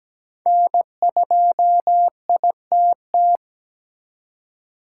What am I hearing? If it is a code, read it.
Morse code, heard as a keyed tone, about 17 wpm, 700 Hz. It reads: N2ITT
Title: USA Call Signs - 5 Characters - 3x Character Spacing - 17wpm